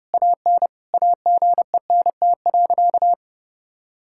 Morse code, heard as a keyed tone, about 32 wpm, 700 Hz. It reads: AN AGENT.
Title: Sentences from Top 1600 Words 30wpm